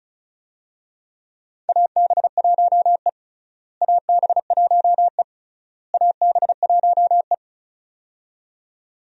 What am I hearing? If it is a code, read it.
Morse code, heard as a keyed tone, about 36 wpm, 700 Hz. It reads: AB1E AB1E AB1E
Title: USA Call Signs - 4 Characters - Repeated 3x - 35wpm